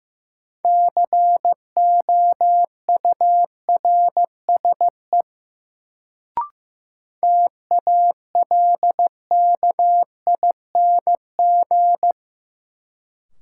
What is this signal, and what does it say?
Morse code, heard as a keyed tone, about 15 wpm, 700 Hz. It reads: COURSE TALKING
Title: Sets of 2 Words - New Word Drill - 15wpm